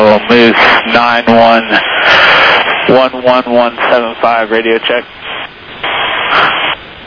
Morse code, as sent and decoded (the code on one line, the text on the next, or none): none